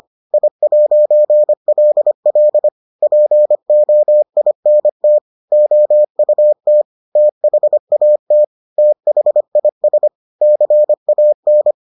Morse code, heard as a keyed tone, about 25 wpm, 600 Hz. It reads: I'LL POINT OUT THAT THIS CAN